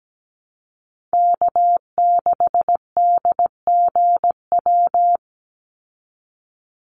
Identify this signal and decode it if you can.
Morse code, heard as a keyed tone, about 17 wpm, 700 Hz. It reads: K6DGW
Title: Call Signs Repeated 3x - 17wpm